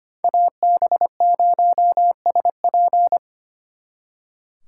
Morse code, heard as a keyed tone, about 30 wpm, 700 Hz. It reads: AB0SP
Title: Speed Racing - USA Call Signs 25wpm